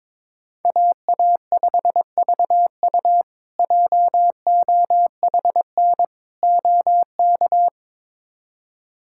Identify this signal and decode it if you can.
Morse code, heard as a keyed tone, about 22 wpm, 700 Hz. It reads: AA5VU JOHN OK
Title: K1USN SST QSO Practice - 22wpm